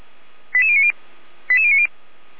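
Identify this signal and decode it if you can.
Five-tone selective call (ZVEI), heard as quick runs of five tones, tones of about 70 ms. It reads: 88090 88090
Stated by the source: ZVEI five-tone selective call 88139 received via SDR.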